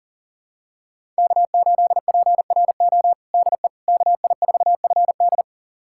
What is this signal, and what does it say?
Morse code, heard as a keyed tone, about 40 wpm, 700 Hz. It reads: K8PRO DE KI4FD
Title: USA Callsign DE Callsign - 5 Characters - 40wpm